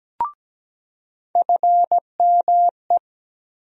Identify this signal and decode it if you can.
Morse code, heard as a keyed tone, about 17 wpm, 700 Hz. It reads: FME
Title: USA Call Sign Postfixes - 3 Letters 17wpm